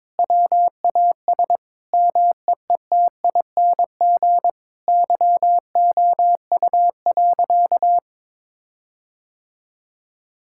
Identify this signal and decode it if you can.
Morse code, heard as a keyed tone, about 22 wpm, 700 Hz. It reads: WAS MEETING YOU.
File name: Sentences with 7-Word Phrases - 22wpm